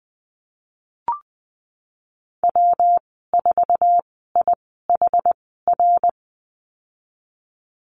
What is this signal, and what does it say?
Morse code, heard as a keyed tone, about 20 wpm, spaced as for 14 wpm, 700 Hz. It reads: W4IHR
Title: 161 Lesson 26g - Letter g - Call Signs 20wpm